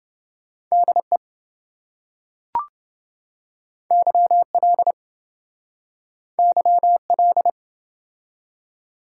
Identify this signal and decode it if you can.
Morse code, heard as a keyed tone, about 26 wpm, 700 Hz. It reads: DE YL YL